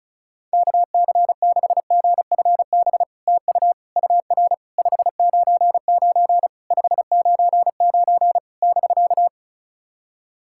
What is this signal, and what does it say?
Morse code, heard as a keyed tone, about 36 wpm, 700 Hz. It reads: KC6GFB TU UR 599 599 <BK>
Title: SOTA QSO Practice 35wpm